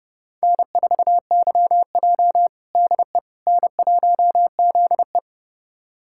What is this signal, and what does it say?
Morse code, heard as a keyed tone, about 30 wpm, 700 Hz. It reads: N4YJ DE N1ZE
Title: USA Callsign DE Callsign - 4 Characters - 30wpm